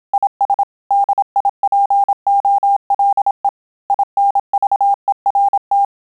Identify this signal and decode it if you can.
Morse code, heard as a keyed tone, about 27 wpm, 800 Hz. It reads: IS DIPOLE INVERT
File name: Telegraphy-group / QSO CW 25 PPM